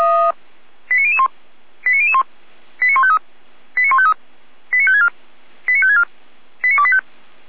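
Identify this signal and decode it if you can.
Five-tone selective call (ZVEI), heard as quick runs of five tones, tones of about 80 ms. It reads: 89001 89001 89153 89153 89564 89564 89285
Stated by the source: ZVEI five-tone selective call received on SDR.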